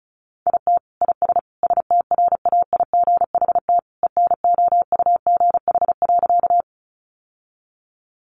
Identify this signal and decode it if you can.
Morse code, heard as a keyed tone, about 34 wpm, 700 Hz. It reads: IT IS STRAIGHT ENOUGH.
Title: Sentences from Top 1000 Words 35wpm